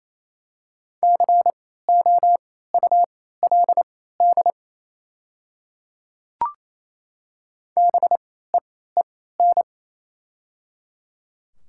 Morse code, heard as a keyed tone, about 28 wpm, spaced as for 16 wpm, 700 Hz. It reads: COULD BEEN
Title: 3x Character Spacing - Top 100 Words - 28wpm